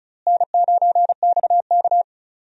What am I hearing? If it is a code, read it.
Morse code, heard as a keyed tone, about 34 wpm, 700 Hz. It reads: N9XK
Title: USA Call Signs - 4 Characters - Repeated 3x - 35wpm